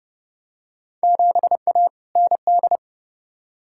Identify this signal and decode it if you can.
Morse code, heard as a keyed tone, about 31 wpm, 700 Hz. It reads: 7A ND
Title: Field Day Exchanges Encore - 30wpm